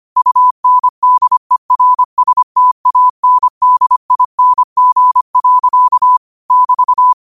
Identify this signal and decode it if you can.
Morse code, heard as a keyed tone, about 25 wpm, 1000 Hz. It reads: UNDERSTANDING. <BT>